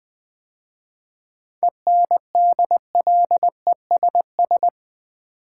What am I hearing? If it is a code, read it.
Morse code, heard as a keyed tone, about 20 wpm, 700 Hz. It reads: ENDLESS
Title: 7 Letter Words - First 3 Letters Given - 20wpm